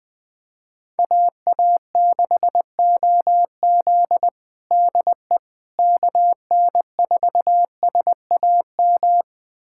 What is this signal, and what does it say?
Morse code, heard as a keyed tone, about 21 wpm, 700 Hz. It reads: AA6OZ DE KN4SAM
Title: USA Callsign DE Callsign 20wpm